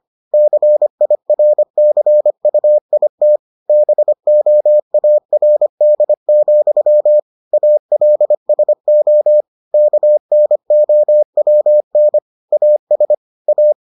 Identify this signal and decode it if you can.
Morse code, heard as a keyed tone, about 25 wpm, 600 Hz. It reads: CIRCUIT BOARD, ALSO KNOWN AS A